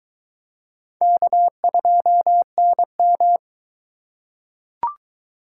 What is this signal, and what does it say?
Morse code, heard as a keyed tone, about 21 wpm, 700 Hz. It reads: K2NM